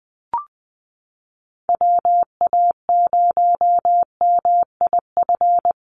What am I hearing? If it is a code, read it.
Morse code, heard as a keyed tone, about 20 wpm, 700 Hz. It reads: WA0MIF
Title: USA Call Signs 20wpm